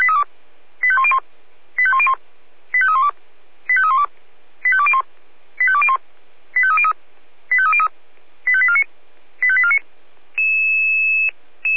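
Five-tone selective call (ZVEI), heard as quick runs of five tones, tones of about 70 ms. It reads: 85101 85101 85312 85312 85201 85201 85303 85303 85849 85849
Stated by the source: ZVEI five-tone selective call received on SDR, address 85259.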